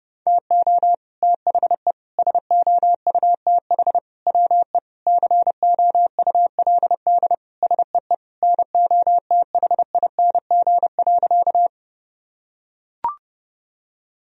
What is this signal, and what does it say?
Morse code, heard as a keyed tone, about 30 wpm, 700 Hz. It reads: TO THE SOUTH WE COULD SEE NOTHING.
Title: Sentences from Top 1000 Words - Review Part 2 - 30wpm